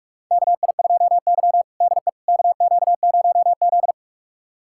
Morse code, heard as a keyed tone, about 45 wpm, 700 Hz. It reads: KI2Y DE KQ0Z
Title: USA Callsign DE Callsign - 4 Characters - 45wpm